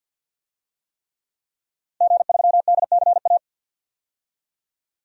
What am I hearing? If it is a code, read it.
Morse code, heard as a keyed tone, about 49 wpm, 700 Hz. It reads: G3DCA
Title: DXCC-100 Call Signs - 5 Characters - Repeated 3x - 50wpm